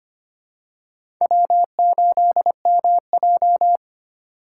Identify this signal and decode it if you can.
Morse code, heard as a keyed tone, about 25 wpm, 700 Hz. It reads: W8MJ
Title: USA Call Signs - 3x Character Spacing - 25wpm